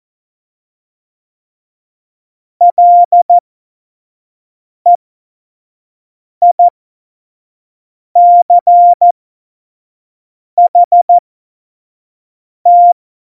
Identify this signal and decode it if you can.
Morse code, heard as a keyed tone, about 14 wpm, spaced as for 5 wpm, 700 Hz. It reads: LEICHT
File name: Morsekurs / Witze / Witz 26 Mai 2020, 25 F70